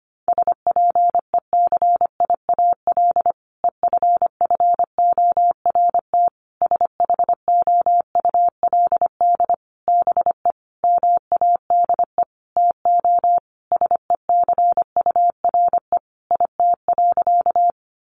Morse code, heard as a keyed tone, about 25 wpm, 700 Hz. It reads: SPECIAL EFFORT SHOULD BE MADE TO SECURE IT.
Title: Sentences from Top 1200 Words 25wpm